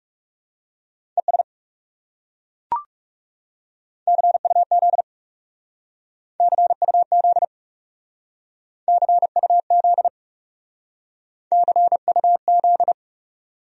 Morse code, heard as a keyed tone, about 38 wpm, 700 Hz. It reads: ES CUZ CUZ CUZ CUZ